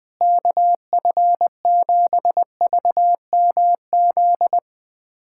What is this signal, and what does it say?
Morse code, heard as a keyed tone, about 20 wpm, 700 Hz. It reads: KF7VMZ
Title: USA Callsign DE Callsign - 6 Characters - 20wpm